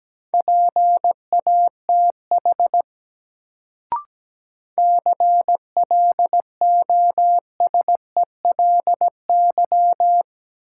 Morse code, heard as a keyed tone, about 17 wpm, 700 Hz. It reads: PATH CLOSELY
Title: Sentences from Top 1400 Words - New Word Drill Alt - 17wpm